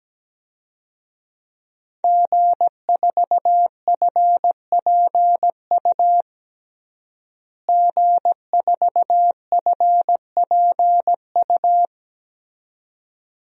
Morse code, heard as a keyed tone, about 17 wpm, 700 Hz. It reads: G4FPU G4FPU
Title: DXCC-100 Call Signs - 5 Characters - Repeated 2x - 17wpm